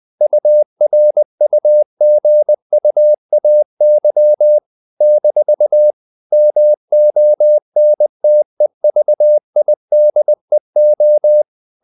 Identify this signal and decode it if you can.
Morse code, heard as a keyed tone, about 20 wpm, 600 Hz. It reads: URUGUAY - MONTEVIDEO
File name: CW / Practice / NationalCapitals20